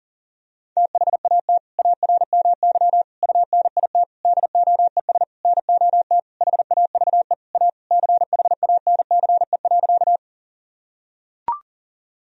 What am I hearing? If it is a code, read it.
Morse code, heard as a keyed tone, about 40 wpm, 700 Hz. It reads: THAT ARMY UNIT DOES NOT HAVE A CHANCE.
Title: Sentences from Top 1000 Words - Review Part 2 - 40wpm